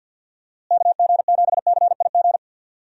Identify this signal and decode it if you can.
Morse code, heard as a keyed tone, about 45 wpm, 700 Hz. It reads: KG7CIG
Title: USA Call Signs - 6 Characters - Repeated 2x - 50wpm